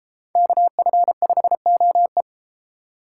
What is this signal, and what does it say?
Morse code, heard as a keyed tone, about 32 wpm, 700 Hz. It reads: KF5OE